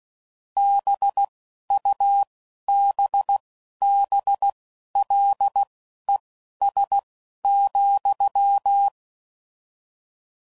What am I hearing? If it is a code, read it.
Morse code, heard as a keyed tone, about 16 wpm, spaced as for 11 wpm, 800 Hz. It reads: BUBBLES,